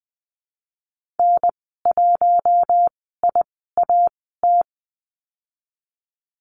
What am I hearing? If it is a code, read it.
Morse code, heard as a keyed tone, about 20 wpm, spaced as for 15 wpm, 700 Hz. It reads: N1IAT
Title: 042 Lesson 8g - Number 1 - Call Signs 20wpm